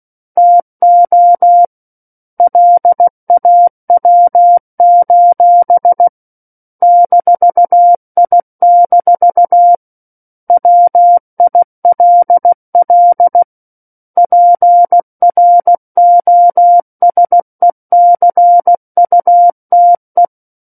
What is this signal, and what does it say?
Morse code, heard as a keyed tone, about 16 wpm, 700 Hz. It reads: TO LAW: -I- WILL PROSECUTE
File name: 03-Chapter-03 16 WPM